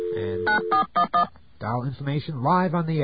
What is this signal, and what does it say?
DTMF, heard as telephone touch tones, tones of about 110 ms, with speech. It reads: 9411
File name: mp3files / off the hook